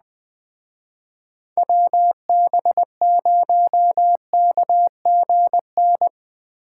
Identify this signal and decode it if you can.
Morse code, heard as a keyed tone, about 19 wpm, 700 Hz. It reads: WB0KGN
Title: Speed Racing - USA Call Signs 17wpm